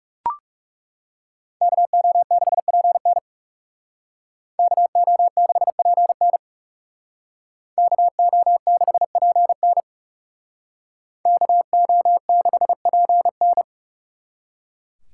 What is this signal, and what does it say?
Morse code, heard as a keyed tone, about 37 wpm, 700 Hz. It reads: KO6PN KO6PN KO6PN KO6PN
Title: Speed Racing - USA 2x2 Call Signs 30wpm